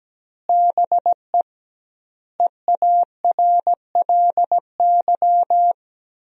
Morse code, heard as a keyed tone, about 17 wpm, 700 Hz. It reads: BE EARLY
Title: 2x Word-Spacing - Sentences from Top 400 Words - 17wpm